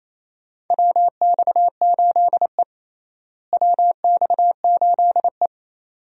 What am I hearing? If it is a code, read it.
Morse code, heard as a keyed tone, about 28 wpm, 700 Hz. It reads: WX8E WX8E
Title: USA Call Signs - 4 Characters - Repeated 2x - 28wpm